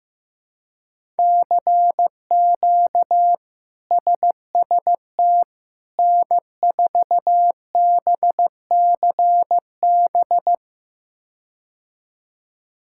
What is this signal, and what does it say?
Morse code, heard as a keyed tone, about 15 wpm, 700 Hz. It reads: CQ SST N4BCB
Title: K1USN SST QSO Practice - 15wpm